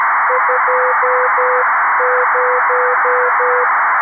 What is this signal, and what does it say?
Morse code, heard as a keyed tone, about 13 wpm, 500 Hz, with noise. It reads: 20
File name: arissat / recordings / ARISSat 11Sept2011 1354utc 60frames